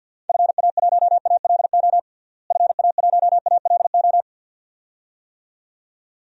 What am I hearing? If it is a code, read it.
Morse code, heard as a keyed tone, about 53 wpm, 700 Hz. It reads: FA1ALO FA1ALO
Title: DXCC-100 Call Signs - 6 Characters - Repeated 2x - 50wpm